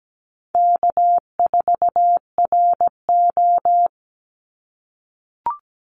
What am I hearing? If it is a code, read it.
Morse code, heard as a keyed tone, about 17 wpm, 700 Hz. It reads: K4RO
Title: Call Signs - 17wpm